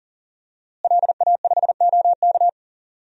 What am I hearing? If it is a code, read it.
Morse code, heard as a keyed tone, about 40 wpm, 700 Hz. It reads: LA5OK
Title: DXCC-100 Call Signs - 5 Characters - Repeated 3x - 40wpm